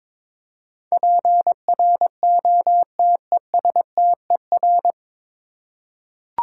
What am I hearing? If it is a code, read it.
Morse code, heard as a keyed tone, about 21 wpm, 700 Hz. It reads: PROTESTER
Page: Morse Code Ninja practice 22wpm